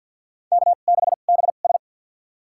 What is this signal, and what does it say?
Morse code, heard as a keyed tone, about 47 wpm, spaced as for 35 wpm, 700 Hz. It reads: K6BS
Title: USA Call Signs - 4 Characters - 2x Character Spacing - 50wpm